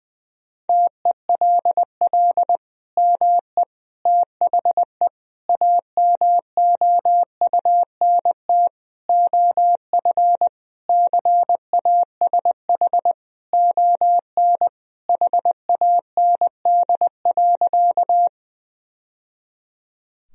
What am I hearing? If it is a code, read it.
Morse code, heard as a keyed tone, about 20 wpm, 700 Hz. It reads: TELL ME THE AMOUNT OF CASH ON HAND.
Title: Sentences from Top 1000 Words - Review Part 4 - 20wpm